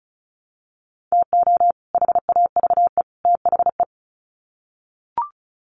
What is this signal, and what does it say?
Morse code, heard as a keyed tone, about 34 wpm, 700 Hz. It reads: TO HAVE THE
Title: Sets of 3 Words 35wpm